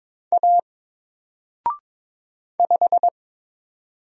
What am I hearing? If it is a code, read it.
Morse code, heard as a keyed tone, about 22 wpm, 700 Hz. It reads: A 5
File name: Single-Letter-Number-22wpm